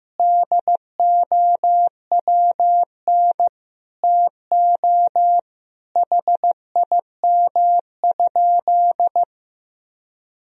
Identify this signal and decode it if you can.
Morse code, heard as a keyed tone, about 15 wpm, 700 Hz. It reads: DOWN TO HIM?
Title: Sentences from Top 1000 Words - Review Part 4 - 15wpm